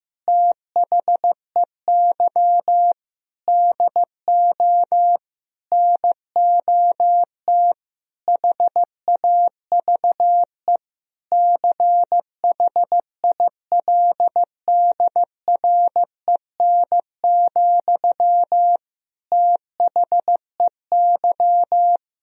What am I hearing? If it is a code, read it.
Morse code, heard as a keyed tone, about 15 wpm, 700 Hz. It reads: THEY DO NOT HAVE CHILDREN, THEY
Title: Sentences from Top 1000 Words - Review Part 3 - 15wpm